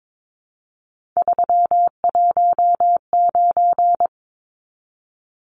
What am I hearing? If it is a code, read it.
Morse code, heard as a keyed tone, about 22 wpm, 700 Hz. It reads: 319
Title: Numbers - 3-Digit 22wpm